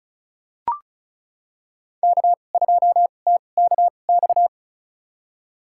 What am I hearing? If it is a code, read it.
Morse code, heard as a keyed tone, about 34 wpm, spaced as for 25 wpm, 700 Hz. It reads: K2TKX
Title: USA Call Signs - 5 Characters - 2x Character Spacing - 35wpm